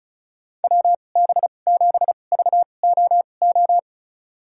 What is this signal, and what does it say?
Morse code, heard as a keyed tone, about 36 wpm, spaced as for 25 wpm, 700 Hz. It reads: WB7VOO
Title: USA Call Signs - 6 Characters - 2x Character Spacing - 35wpm